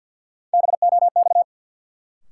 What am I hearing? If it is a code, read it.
Morse code, heard as a keyed tone, about 52 wpm, 700 Hz. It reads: BOX
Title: Sentences from Top 500 Words - New Word Drill - 50wpm